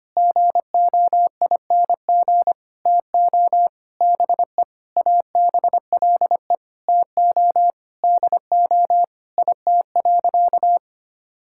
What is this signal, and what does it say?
Morse code, heard as a keyed tone, about 25 wpm, 700 Hz. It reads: GOING TO BE ABLE TO DO IT.
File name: practice / 25wpm / Sentences with 4-Word Phrases - 25wpm